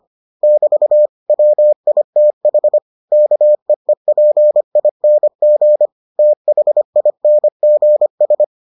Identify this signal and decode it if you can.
Morse code, heard as a keyed tone, about 25 wpm, 600 Hz. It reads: <BT> WITH KEEPING THINGS